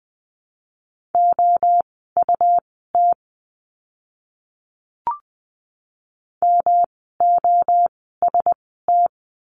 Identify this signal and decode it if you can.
Morse code, heard as a keyed tone, about 20 wpm, spaced as for 15 wpm, 700 Hz. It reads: OUT MOST
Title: 237 Lesson 38f - Letter q - Words 20wpm